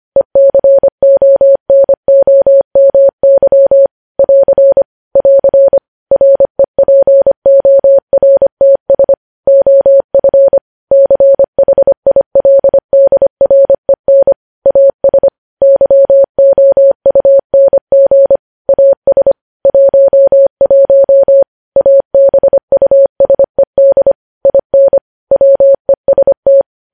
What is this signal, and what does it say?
Morse code, heard as a keyed tone, about 25 wpm, 550 Hz. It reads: ECONOMY <AR> <AR> REPORTS OF CHILDREN AS YOUNG AS 11 ABUSED IN WEST
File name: podcast / News headlines-1x-25x25